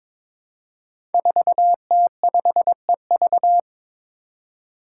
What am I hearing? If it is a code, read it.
Morse code, heard as a keyed tone, about 22 wpm, 700 Hz. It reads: 4T5EV
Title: Five Challenging Letter-Number Combinations 22wpm